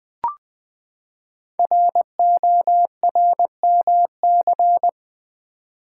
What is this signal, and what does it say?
Morse code, heard as a keyed tone, about 20 wpm, 700 Hz. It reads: RORMC
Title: Code Groups - 5 Letters - 20wpm